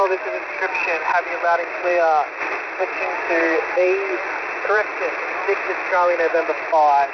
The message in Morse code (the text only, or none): none